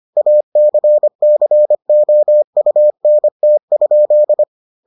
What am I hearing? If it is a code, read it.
Morse code, heard as a keyed tone, about 25 wpm, 600 Hz. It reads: ACCOUNT?